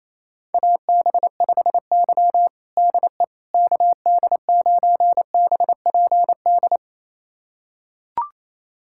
Morse code, heard as a keyed tone, about 28 wpm, 700 Hz. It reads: AB5Y DE KD9BPD
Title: USA Callsign DE Callsign 28wpm